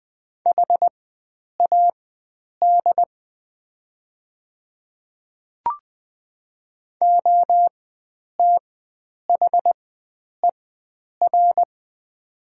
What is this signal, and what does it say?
Morse code, heard as a keyed tone, about 20 wpm, spaced as for 9 wpm, 700 Hz. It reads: HAD OTHER
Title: 4x Character Spacing - Top 100 Words - 20wpm